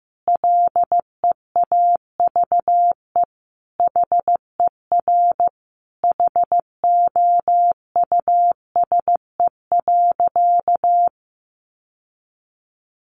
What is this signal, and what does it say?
Morse code, heard as a keyed tone, about 15 wpm, 700 Hz. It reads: LEAVE HER HOUSE.
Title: Sentences from Top 1500 Words 15wpm